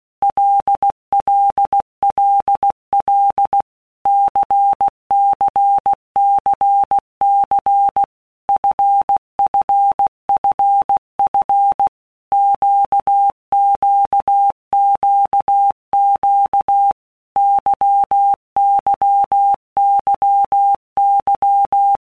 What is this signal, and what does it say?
Morse code, heard as a keyed tone, about 16 wpm, 800 Hz. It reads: LLLL CCCC FFFF QQQQ YYYY
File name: Basic Letter Groups By Blocks / Groups@15 / LFCYQ@15